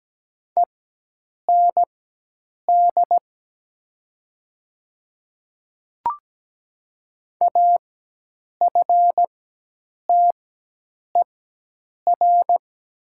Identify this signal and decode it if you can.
Morse code, heard as a keyed tone, about 17 wpm, spaced as for 8 wpm, 700 Hz. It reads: END AFTER
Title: 4x Character Spacing - Top 500 Words - 17wpm